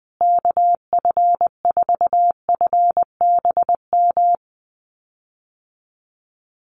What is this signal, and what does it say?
Morse code, heard as a keyed tone, about 20 wpm, 700 Hz. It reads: KF4FBM
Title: USA Call Signs Repeated 2x 20wpm